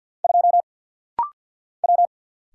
Morse code, heard as a keyed tone, about 49 wpm, 700 Hz. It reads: 2 W
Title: Single Letter-Number - Rapid-Fire 50wpm